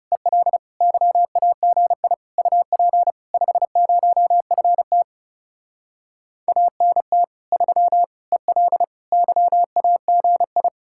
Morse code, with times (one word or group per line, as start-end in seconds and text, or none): 0.1-0.6 EL
0.8-2.1 YAGI
2.4-3.1 UP
3.3-5.0 50FT
6.5-7.2 ANT
7.5-8.0 3
8.3-8.8 EL
9.1-10.7 YAGI